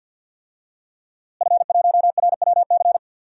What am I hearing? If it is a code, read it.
Morse code, heard as a keyed tone, about 50 wpm, 700 Hz. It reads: F1RWC
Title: DXCC-100 Call Signs - 5 Characters - Repeated 3x - 50wpm